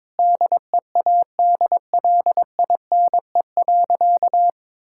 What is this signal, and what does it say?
Morse code, heard as a keyed tone, about 22 wpm, 700 Hz. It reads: DEADLINE.
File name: practice / 22wpm / Sentences with 8-Word Phrases - 22wpm